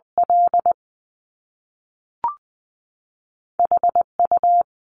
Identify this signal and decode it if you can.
Morse code, heard as a keyed tone, about 20 wpm, spaced as for 3 wpm, 700 Hz. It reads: LH U
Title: Top Two Letter Combinations in Words 20wpm